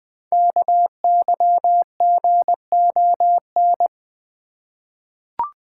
Morse code, heard as a keyed tone, about 20 wpm, 700 Hz. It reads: KYGON
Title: Code Groups - 5 Letters - 20wpm